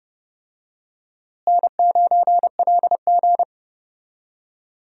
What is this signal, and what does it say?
Morse code, heard as a keyed tone, about 32 wpm, 700 Hz. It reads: N9LG
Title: USA Call Signs - 4 Characters - Repeated 2x - 30wpm